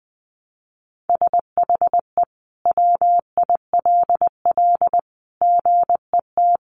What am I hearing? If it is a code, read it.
Morse code, heard as a keyed tone, about 19 wpm, 700 Hz. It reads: SHE WILL GET
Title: Sentences from Top 1200 Words 20wpm